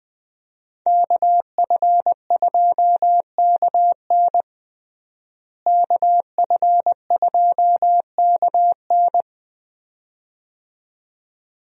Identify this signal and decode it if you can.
Morse code, heard as a keyed tone, about 20 wpm, 700 Hz. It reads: KF2KN KF2KN
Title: USA Call Signs - 5 Characters - Repeated 2x - 20wpm